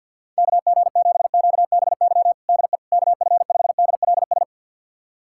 Morse code, heard as a keyed tone, about 47 wpm, 700 Hz. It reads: KG7QBY DE KF5DLS
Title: USA Callsign DE Callsign - 6 Characters - 50wpm